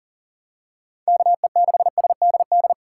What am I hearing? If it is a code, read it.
Morse code, heard as a keyed tone, about 40 wpm, 700 Hz. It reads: KE6SDD
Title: USA Call Signs - 6 Characters - 6x Character Spacing - 40wpm